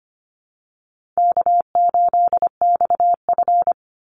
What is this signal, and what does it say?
Morse code, heard as a keyed tone, about 25 wpm, 700 Hz. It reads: K8XF
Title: Easy Call Signs Repeated 2x - 25wpm